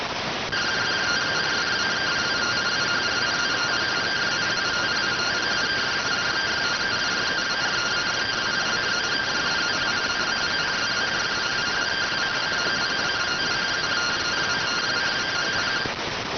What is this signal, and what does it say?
Two-tone teleprinter signal, about 170 Hz shift.